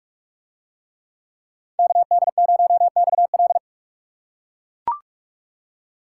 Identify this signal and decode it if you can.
Morse code, heard as a keyed tone, about 42 wpm, 700 Hz. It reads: KD0XL